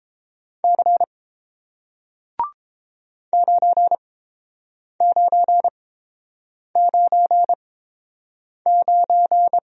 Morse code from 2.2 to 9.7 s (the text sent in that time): 9999